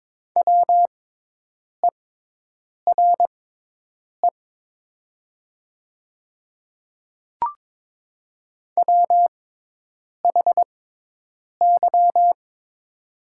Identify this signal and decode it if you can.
Morse code, heard as a keyed tone, about 22 wpm, spaced as for 8 wpm, 700 Hz. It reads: WERE WHY
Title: 6x Character Spacing - Top 200 Words - 22wpm